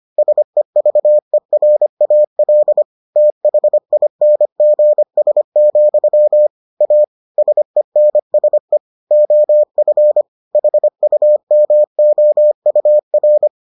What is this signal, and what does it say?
Morse code, heard as a keyed tone, about 25 wpm, 600 Hz. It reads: SEVERAL THINGS, A SENSE OF HUMOUR